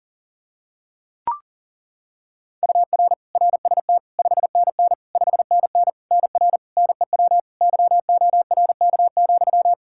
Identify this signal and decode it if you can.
Morse code, heard as a keyed tone, about 40 wpm, 700 Hz. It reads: UR RST 5NN 5NN NR NEW YORK,
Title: Practice QSO - Exchange 1 - Beginning - 40wpm